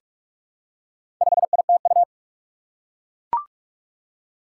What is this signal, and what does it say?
Morse code, heard as a keyed tone, about 44 wpm, 700 Hz. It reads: 5ITU
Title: Four Challenging Letter-Number Combinations 45wpm